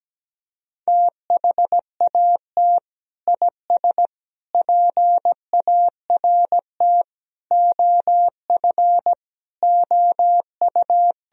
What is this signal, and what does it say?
Morse code, heard as a keyed tone, about 17 wpm, 700 Hz. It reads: THAT IS PART OF OU
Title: Sentences from Top 1000 Words - Review Part 1 - 17wpm